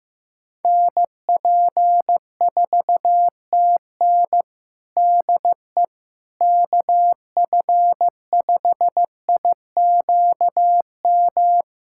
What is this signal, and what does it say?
Morse code, heard as a keyed tone, about 14 wpm, 700 Hz. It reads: NP4TN DE KF5IQM
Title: USA Callsign DE Callsign 15wpm